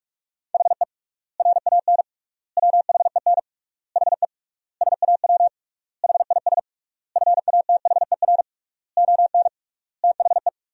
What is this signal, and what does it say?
Morse code, heard as a keyed tone, about 46 wpm, 700 Hz. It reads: HE RAN WHEN HE SAW HIS FATHER ON THE
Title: 3x Word-Spacing - Sentences from Top 500 Words - 45wpm